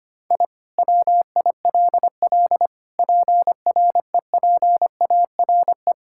Morse code, heard as a keyed tone, about 25 wpm, 700 Hz. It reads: I WILL PREPARE
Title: Sentences from Top 1000 Words - Review Part 6 - 25wpm